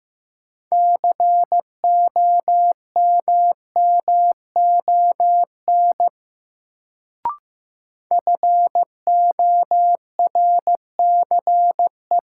Morse code, heard as a keyed tone, about 15 wpm, 700 Hz. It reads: COMMON FORCE
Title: Sentences from Top 700 Words - New Word Drill - 15wpm